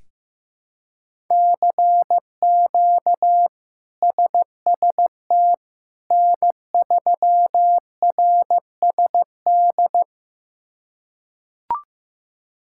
Morse code, heard as a keyed tone, about 15 wpm, 700 Hz. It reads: CQ SST N3RSD
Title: K1USN SST QSO Practice - 15wpm